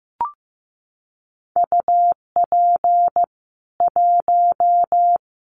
Morse code, heard as a keyed tone, about 15 wpm, 700 Hz. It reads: UP 1
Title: Common QSO Phrases 15wpm